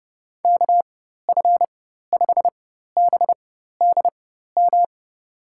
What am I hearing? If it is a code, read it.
Morse code, heard as a keyed tone, about 30 wpm, spaced as for 14 wpm, 700 Hz. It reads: KF5BDM
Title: USA Call Signs - 6 Characters - 4x Character Spacing - 30wpm